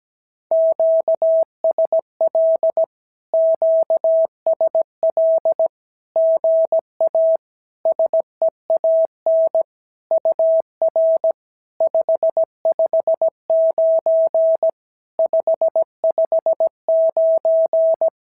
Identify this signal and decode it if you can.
Morse code, heard as a keyed tone, about 17 wpm, 650 Hz. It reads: QSL QSL GA SEAN UR 559 559
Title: SOTA QSO Practice 17wpm